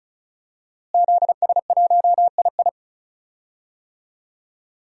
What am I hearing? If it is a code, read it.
Morse code, heard as a keyed tone, about 34 wpm, 700 Hz. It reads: ZS1II
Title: DXCC-100 Call Signs - 5 Characters - Repeated 3x - 35wpm